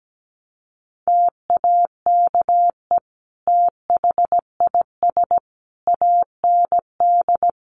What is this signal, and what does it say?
Morse code, heard as a keyed tone, about 17 wpm, 700 Hz. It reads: TAKE THIS AND
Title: Sentences from Top 200 Words 17wpm